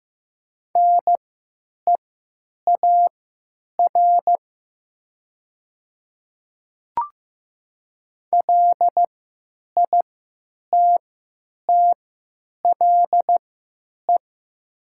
Morse code, heard as a keyed tone, about 15 wpm, spaced as for 9 wpm, 700 Hz. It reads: NEAR LITTLE
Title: 3x Character Spacing - Top 300 Words - 15wpm